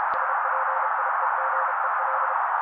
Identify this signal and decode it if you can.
Morse code, heard as a keyed tone, about 32 wpm, 530 Hz. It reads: SPIEGEL
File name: daily / 31wpm noise